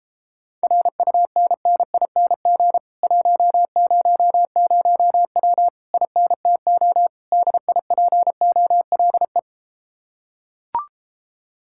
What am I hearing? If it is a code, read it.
Morse code, heard as a keyed tone, about 33 wpm, 700 Hz. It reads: RUNNING 100W INTO DIPOLE